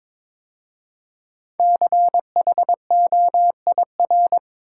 Morse code, heard as a keyed tone, about 22 wpm, 700 Hz. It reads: CHOIR
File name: 5 Letter Words - First 3 Letters Given - 22wpm